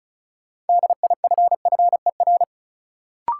Morse code, heard as a keyed tone, about 36 wpm, 700 Hz. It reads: DIFFER
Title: Sentences from Top 1000 Words - Review Part 1 - New Word Drill - 35wpm